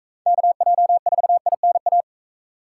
Morse code, heard as a keyed tone, about 41 wpm, 700 Hz. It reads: KJ4INA